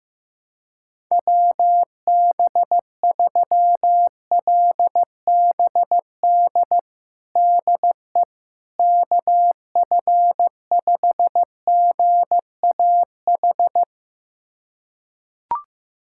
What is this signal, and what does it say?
Morse code, heard as a keyed tone, about 16 wpm, 700 Hz. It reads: WB3LBD DE KF5GAH
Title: USA Callsign DE Callsign 15wpm